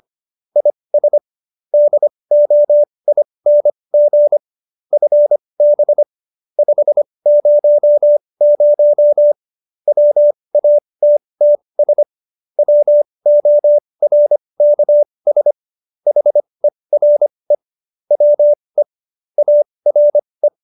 Morse code, heard as a keyed tone, about 25 wpm, spaced as for 20 wpm, 590 Hz. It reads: IS DOING FB 500 WATTS WORKS HERE WE ARE